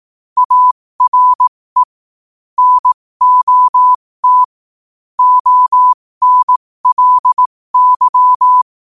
Morse code, heard as a keyed tone, about 18 wpm, spaced as for 15 wpm, 1000 Hz. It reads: ARE NOT ONLY
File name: qotdfiles / 15wpm